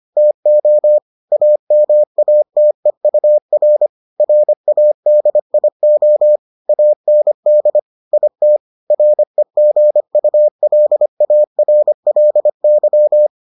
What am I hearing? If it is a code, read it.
Morse code, heard as a keyed tone, about 25 wpm, 600 Hz. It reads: TO AMATEUR RADIO AND IT REGULARLY